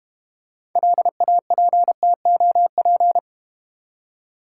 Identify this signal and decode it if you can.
Morse code, heard as a keyed tone, about 29 wpm, 690 Hz. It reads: LAPTOP
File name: Easy Words - Speed-Racing with Hints - 25wpm